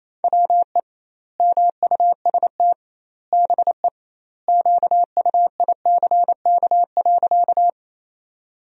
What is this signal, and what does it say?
Morse code, heard as a keyed tone, about 28 wpm, 700 Hz. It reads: WE MUST BE QUICK.
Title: 2x Word-Spacing - Sentences from Top 500 Words - 28wpm